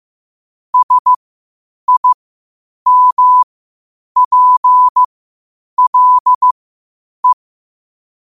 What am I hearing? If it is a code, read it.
Morse code, heard as a keyed tone, about 15 wpm, spaced as for 8 wpm, 1000 Hz. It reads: SIMPLE
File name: qotdfiles / 8wpm